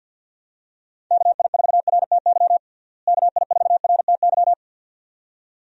Morse code, heard as a keyed tone, about 50 wpm, 700 Hz. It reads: KI4RTY KI4RTY
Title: USA Call Signs - 6 Characters - Repeated 2x - 50wpm